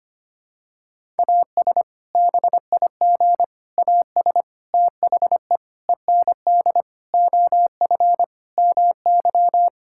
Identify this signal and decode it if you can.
Morse code, heard as a keyed tone, about 25 wpm, 700 Hz. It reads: AS BIG AS THE END OF MY
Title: Sentences from Top 1600 Words 25wpm